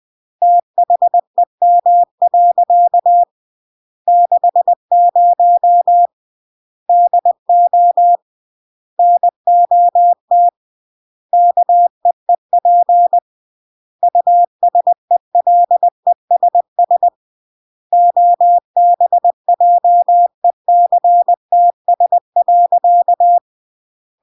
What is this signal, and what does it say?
Morse code, heard as a keyed tone, about 20 wpm, 700 Hz. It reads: THEM. 60 DO NOT KEEP USELESS OBJECTS.
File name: CW / jodorowsky / 82 commandments 20-20